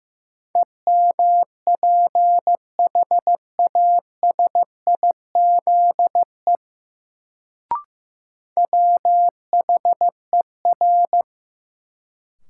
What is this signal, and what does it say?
Morse code, heard as a keyed tone, about 15 wpm, 700 Hz. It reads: EMPHASIZE WHER
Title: Sentences from Top 1500 Words - New Word Drill Alt - 15wpm